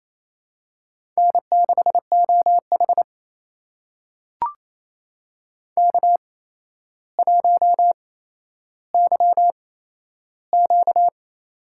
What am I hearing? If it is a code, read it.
Morse code, heard as a keyed tone, about 28 wpm, spaced as for 12 wpm, 700 Hz. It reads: N6OH K1YQ
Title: USA Call Signs - 4 Characters - 8x Character Spacing - 28wpm